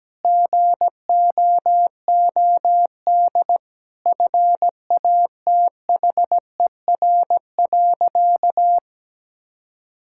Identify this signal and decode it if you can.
Morse code, heard as a keyed tone, about 17 wpm, 700 Hz. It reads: GOOD FATHER.
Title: Sentences from Top 1000 Words - Review Part 5 - 17wpm